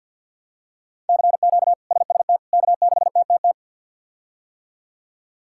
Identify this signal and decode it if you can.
Morse code, heard as a keyed tone, about 51 wpm, 700 Hz. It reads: CQ SST K6TTT
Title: K1USN SST QSO Practice - 50wpm